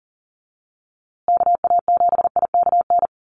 Morse code, heard as a keyed tone, about 40 wpm, 700 Hz. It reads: KA7IKN
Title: USA Call Signs Repeated 2x 40wpm